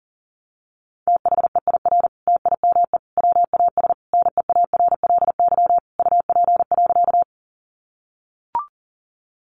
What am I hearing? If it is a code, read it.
Morse code, heard as a keyed tone, about 40 wpm, 700 Hz. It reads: THEIR TIME WAS NEARLY UP.
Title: Sentences from Top 1300 Words 40wpm